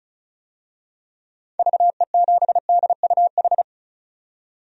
Morse code, heard as a keyed tone, about 36 wpm, 700 Hz. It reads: VE7DUH
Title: DXCC-100 Call Signs - 6 Characters - 35wpm